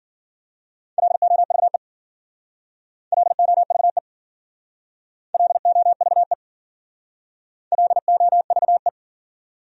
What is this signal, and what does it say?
Morse code, heard as a keyed tone, about 53 wpm, 700 Hz. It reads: LOVE LOVE LOVE LOVE